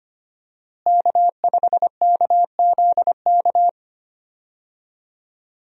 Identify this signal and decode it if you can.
Morse code, heard as a keyed tone, about 25 wpm, 700 Hz. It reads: K5KZK
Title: USA Call Signs - 5 Characters - Repeated 3x - 25wpm